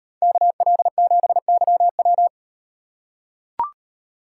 Morse code, heard as a keyed tone, about 36 wpm, 700 Hz. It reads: KL7YW